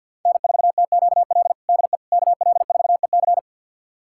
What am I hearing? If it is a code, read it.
Morse code, heard as a keyed tone, about 52 wpm, 700 Hz. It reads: N4TQL DE KL4EC
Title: USA Callsign DE Callsign - 5 Characters - 50wpm